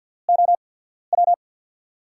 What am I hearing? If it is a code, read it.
Morse code, heard as a keyed tone, about 53 wpm, spaced as for 14 wpm, 700 Hz. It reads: OW
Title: USA Call Signs - 6 Characters - 8x Character Spacing - 50wpm